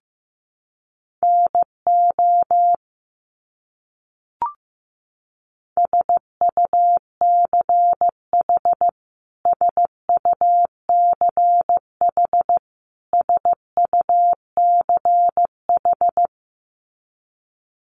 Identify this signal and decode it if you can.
Morse code, heard as a keyed tone, about 14 wpm, 700 Hz. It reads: NO SUCH SUCH SUCH
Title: Top 200 Words Repeated 15wpm